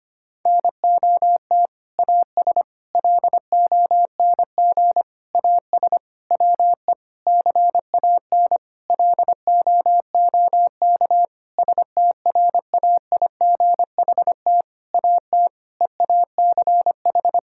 Morse code, heard as a keyed tone, about 25 wpm, 700 Hz. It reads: NOT AS LONG AS WE CAN LOOK STRAIGHT AT EACH
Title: Sentences from Top 1000 Words - Review Part 6 - 25wpm